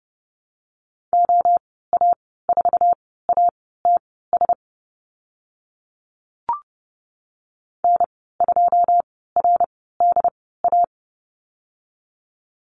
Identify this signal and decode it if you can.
Morse code, heard as a keyed tone, about 30 wpm, spaced as for 17 wpm, 700 Hz. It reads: OA4ATS N2RDA
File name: course / 30:10wpm / 088 Lesson 15g - Number 5 - Call Signs 30wpm